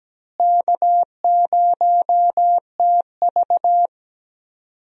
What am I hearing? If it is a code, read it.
Morse code, heard as a keyed tone, about 17 wpm, 700 Hz. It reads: K0TV
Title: USA Call Signs - 4 Characters - Repeated 3x - 17wpm